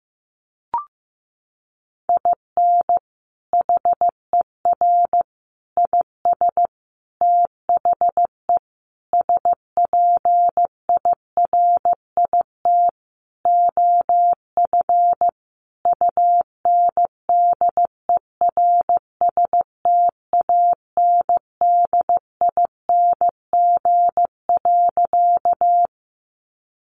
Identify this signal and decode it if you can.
Morse code, heard as a keyed tone, about 15 wpm, 700 Hz. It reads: IN HER IS THE SPIRIT OF UNDERSTANDING.
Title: Sentences from Top 1400 Words 15wpm